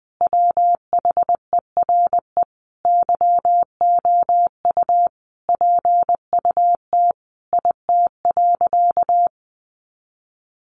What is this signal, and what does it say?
Morse code, heard as a keyed tone, about 20 wpm, 700 Hz. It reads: WHERE YOU PUT IT.
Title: Sentences from Top 200 Words 20wpm